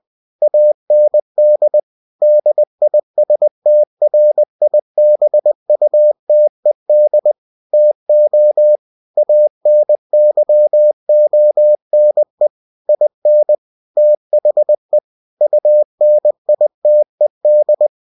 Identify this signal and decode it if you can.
Morse code, heard as a keyed tone, about 20 wpm, 600 Hz. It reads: AND DISTRIBUTED TO ANYONE IN THE UNITED
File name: winnie-20wpm